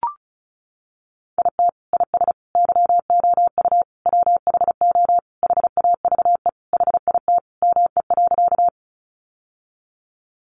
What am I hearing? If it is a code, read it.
Morse code, heard as a keyed tone, about 34 wpm, 700 Hz. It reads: IT IS YOU WHO HAVE HIT ME.
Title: Sentences from Top 1200 Words 35wpm